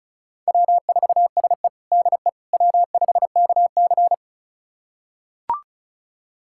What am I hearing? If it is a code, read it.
Morse code, heard as a keyed tone, about 34 wpm, 700 Hz. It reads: W4SE DE W5KC
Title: USA Callsign DE Callsign - 4 Characters - 35wpm